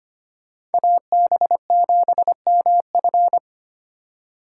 Morse code, heard as a keyed tone, about 32 wpm, 700 Hz. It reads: AB7MF